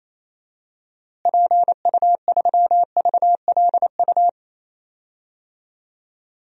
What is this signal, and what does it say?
Morse code, heard as a keyed tone, about 28 wpm, 700 Hz. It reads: PU3VLU
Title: DXCC-100 Call Signs - 6 Characters - Repeated 2x - 28wpm